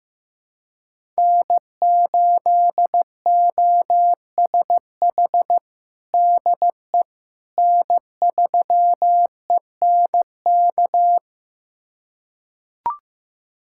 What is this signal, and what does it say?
Morse code, heard as a keyed tone, about 15 wpm, 700 Hz. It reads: N8OSH DE N3ENK
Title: USA Callsign DE Callsign - 5 Characters - 15wpm